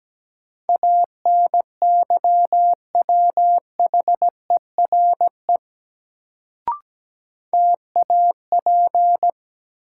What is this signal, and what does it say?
Morse code, heard as a keyed tone, about 17 wpm, 700 Hz. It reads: ANYWHERE TAP
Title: Sentences from Top 1400 Words - New Word Drill - 17wpm